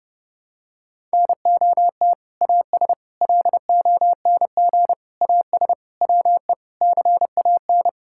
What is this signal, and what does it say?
Morse code, heard as a keyed tone, about 32 wpm, 700 Hz. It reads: NOT AS LONG AS WE CAN
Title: Sentences from Top 1000 Words - Review Part 6 - 30wpm